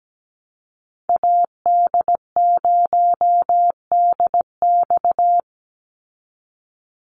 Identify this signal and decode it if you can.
Morse code, heard as a keyed tone, about 17 wpm, 700 Hz. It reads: AD0DX
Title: Call Signs Repeated 3x - 17wpm